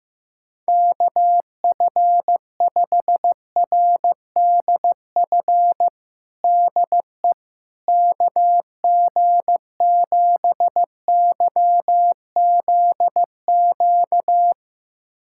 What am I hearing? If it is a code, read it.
Morse code, heard as a keyed tone, about 15 wpm, 700 Hz. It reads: KF5RDF DE KG7YZQ
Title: USA Callsign DE Callsign - 6 Characters - 15wpm